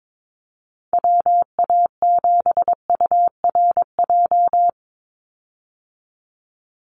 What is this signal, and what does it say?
Morse code, heard as a keyed tone, about 22 wpm, 700 Hz. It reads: WA7URJ
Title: USA Call Signs 22wpm